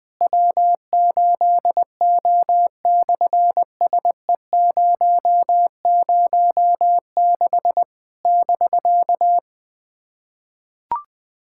Morse code, heard as a keyed tone, about 20 wpm, 700 Hz. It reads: W8O/SE006 <BK>
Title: SOTA QSO Practice 20wpm